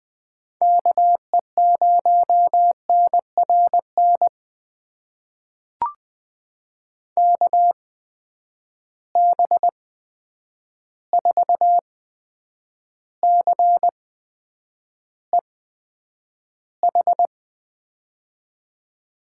Morse code, heard as a keyed tone, about 20 wpm, spaced as for 9 wpm, 700 Hz. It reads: KE0NRN KB4CEH
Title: USA Call Signs - 6 Characters - 8x Character Spacing - 20wpm